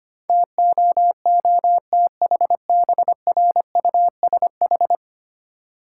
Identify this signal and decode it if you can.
Morse code, heard as a keyed tone, about 25 wpm, 690 Hz. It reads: TOOTHBRUSH
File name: practice / 25wpm / Easy Words with Hints - 25wpm